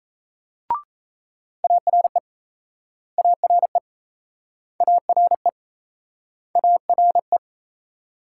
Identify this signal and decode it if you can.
Morse code, heard as a keyed tone, about 39 wpm, 700 Hz. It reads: ARE ARE ARE ARE